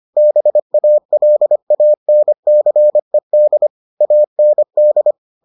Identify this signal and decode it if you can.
Morse code, heard as a keyed tone, about 25 wpm, 600 Hz. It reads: BALANCED AND